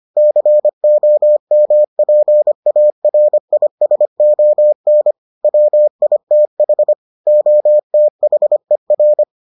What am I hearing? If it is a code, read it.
Morse code, heard as a keyed tone, about 25 wpm, 600 Hz. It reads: COMPARISON WITH OTHER